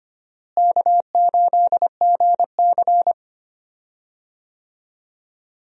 Morse code, heard as a keyed tone, about 25 wpm, 700 Hz. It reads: K8GC
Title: USA Call Signs - 4 Characters - Repeated 3x - 25wpm